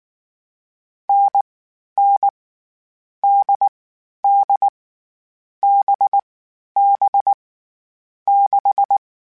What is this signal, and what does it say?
Morse code, heard as a keyed tone, about 19 wpm, 800 Hz. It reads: N N D D B B 6
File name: Short Runs / Short Runs-mp3s / NDB NDB6 @18